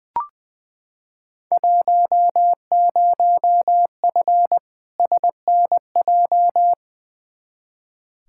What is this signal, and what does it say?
Morse code, heard as a keyed tone, about 20 wpm, 700 Hz. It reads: 10F SNJ
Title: Field Day Exchanges Encore - 20wpm